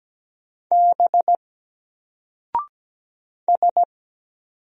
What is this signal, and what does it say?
Morse code, heard as a keyed tone, about 17 wpm, 700 Hz. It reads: B S
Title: Single Letter-Number 17wpm